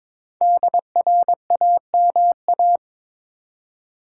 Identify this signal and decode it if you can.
Morse code, heard as a keyed tone, about 22 wpm, 700 Hz. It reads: DRAMA
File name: 5 Letter Words - First 3 Letters Given - 22wpm